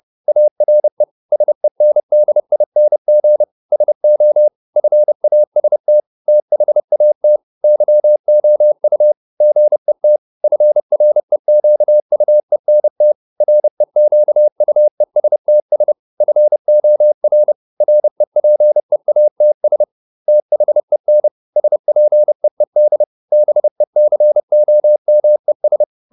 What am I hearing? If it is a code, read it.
Morse code, heard as a keyed tone, about 30 wpm, 600 Hz. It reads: ARE SENDING SO FAST THAT YOU GET FREQUENT REQUESTS FOR REPEATS THEN SPEED BECOMES